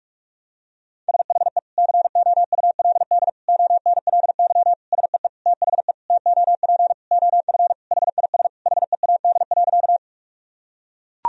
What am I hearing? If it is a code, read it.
Morse code, heard as a keyed tone, about 42 wpm, 700 Hz. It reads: SHE COULD ONLY SEE THE TOP OF HIS HEAD.
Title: Sentences from Top 1000 Words - Review Part 2 - 45wpm